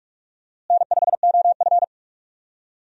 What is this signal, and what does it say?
Morse code, heard as a keyed tone, about 42 wpm, 700 Hz. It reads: N5OF
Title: USA Call Signs - 4 Characters - Repeated 3x - 45wpm